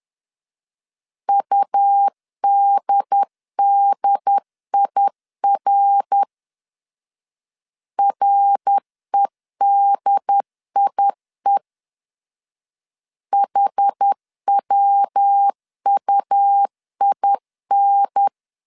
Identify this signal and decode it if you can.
Morse code, heard as a keyed tone, about 10 wpm, 800 Hz. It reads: UDDIR REDIE HWUIN
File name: CW / DET / 1 3 1500C S B 10wpm 3335